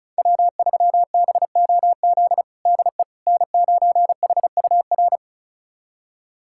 Morse code, heard as a keyed tone, about 36 wpm, 700 Hz. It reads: W3BOZ DE N9HUR
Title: USA Callsign DE Callsign - 5 Characters - 35wpm